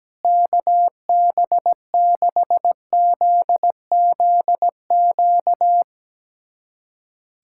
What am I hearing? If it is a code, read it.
Morse code, heard as a keyed tone, about 17 wpm, 700 Hz. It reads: KB6ZZQ
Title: USA Callsign DE Callsign - 6 Characters - 17wpm